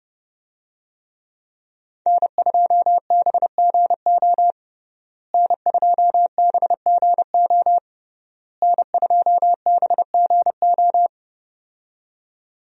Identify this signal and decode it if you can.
Morse code, heard as a keyed tone, about 30 wpm, 700 Hz. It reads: N2BGO N2BGO N2BGO
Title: USA Call Signs - 5 Characters - Repeated 3x - 30wpm